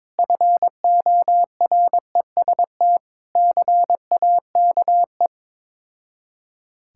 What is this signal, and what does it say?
Morse code, heard as a keyed tone, about 21 wpm, 690 Hz. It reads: FOREST CAKE
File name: Difficult Words with Hints - 22wpm